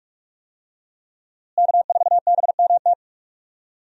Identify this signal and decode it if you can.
Morse code, heard as a keyed tone, about 46 wpm, 700 Hz. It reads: K4BMT
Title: USA Call Signs - 5 Characters - Repeated 3x - 45wpm